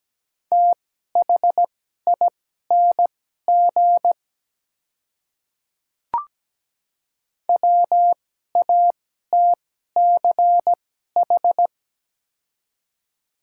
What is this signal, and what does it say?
Morse code, heard as a keyed tone, about 17 wpm, spaced as for 12 wpm, 700 Hz. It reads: THING WATCH
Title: 2x Character Spacing - Top 300 Words - 17wpm